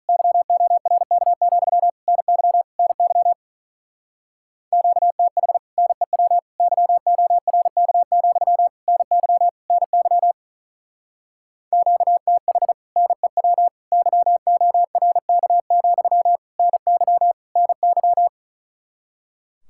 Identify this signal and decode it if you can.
Morse code, heard as a keyed tone, about 44 wpm, 700 Hz. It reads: YORK, NY NY QTH NEW YORK, NY NY QTH NEW YORK, NY NY